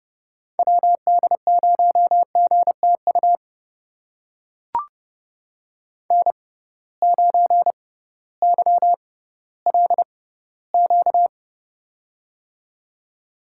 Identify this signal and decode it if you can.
Morse code, heard as a keyed tone, about 28 wpm, spaced as for 16 wpm, 700 Hz. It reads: WD0GTU N9YLQ
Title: USA Call Signs - 6x Character Spacing - 30wpm